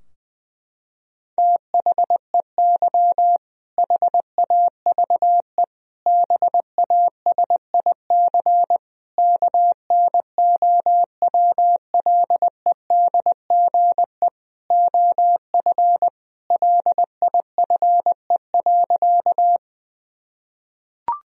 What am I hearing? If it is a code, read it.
Morse code, heard as a keyed tone, about 20 wpm, 700 Hz. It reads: THEY HAVE BASIC KNOWLEDGE OF LIFE.
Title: Sentences from Top 1000 Words - Review Part 3 - 20wpm